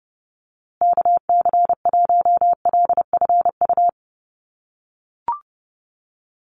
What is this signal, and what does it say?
Morse code, heard as a keyed tone, about 30 wpm, 700 Hz. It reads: KC1LFU
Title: USA Call Signs Repeated 2x 30wpm